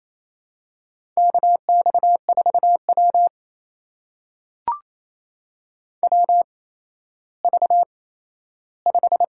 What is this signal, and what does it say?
Morse code, heard as a keyed tone, about 28 wpm, spaced as for 14 wpm, 700 Hz. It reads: KX4W WV5
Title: USA Call Signs - 4 Characters - 8x Character Spacing - 28wpm